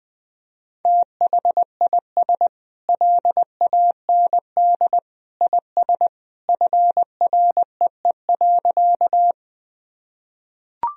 Morse code, heard as a keyed tone, about 20 wpm, 700 Hz. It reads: THIS LAND IS FREE.
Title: Sentences from Top 1000 Words - Review Part 1 - 20wpm